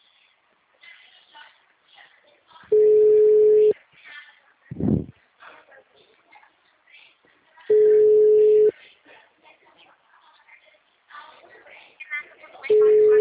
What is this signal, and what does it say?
Ringback tone (Europe, 425 Hz), with speech.